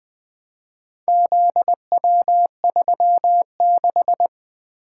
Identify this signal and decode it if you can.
Morse code, heard as a keyed tone, about 24 wpm, 700 Hz. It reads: ZW36
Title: Speed-Racing - Four Letter-Number Combinations - 20wpm